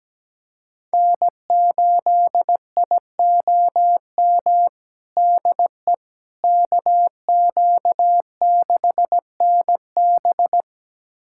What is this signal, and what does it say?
Morse code, heard as a keyed tone, about 17 wpm, 700 Hz. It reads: N8IOM DE KQ6NB
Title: USA Callsign DE Callsign - 5 Characters - 17wpm